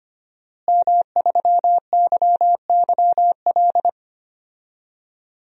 Morse code, heard as a keyed tone, about 25 wpm, 700 Hz. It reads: M3YYL
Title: DXCC-100 Call Signs - 5 Characters - Repeated 2x - 25wpm